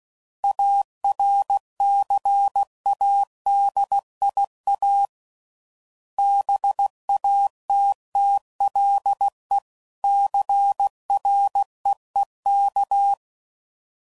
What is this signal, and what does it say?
Morse code, heard as a keyed tone, about 16 wpm, 800 Hz. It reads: ARCADIA BATTLE CREEK
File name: CW Code Groups @15WPM / fromSM's / Cities